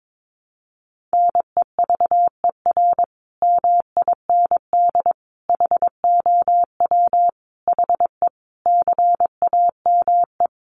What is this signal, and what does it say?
Morse code, heard as a keyed tone, about 22 wpm, 700 Hz. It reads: NEVER MIND HOW HE CAME
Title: Sentences from Top 500 Words 22wpm